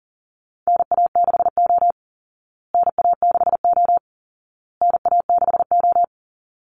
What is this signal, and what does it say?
Morse code, heard as a keyed tone, about 40 wpm, 700 Hz. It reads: NA6O NA6O NA6O
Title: Call Signs Repeated 2x - 40wpm